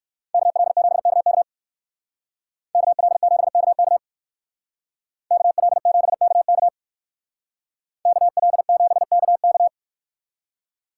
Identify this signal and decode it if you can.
Morse code, heard as a keyed tone, about 58 wpm, 700 Hz. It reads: KL7KK KL7KK KL7KK KL7KK